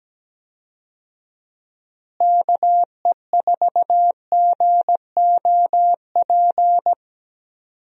Morse code, heard as a keyed tone, about 17 wpm, 700 Hz. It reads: KE4GOP
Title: USA Call Signs - 6 Characters - Repeated 3x - 17wpm